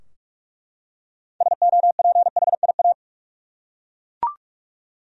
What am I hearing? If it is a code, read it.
Morse code, heard as a keyed tone, about 41 wpm, 700 Hz. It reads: SOPHIA
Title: USA Common Names - 45wpm